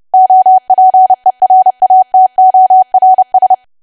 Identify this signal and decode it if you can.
Morse code, heard as a keyed tone, about 30 wpm, 750 Hz. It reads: OPERATORS
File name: cwa / W1AW-Level3 / 30WPM